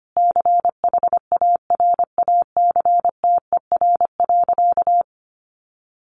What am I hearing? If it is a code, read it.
Morse code, heard as a keyed tone, about 25 wpm, 700 Hz. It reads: CHARACTER.
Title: Sentences from Top 1500 Words 25wpm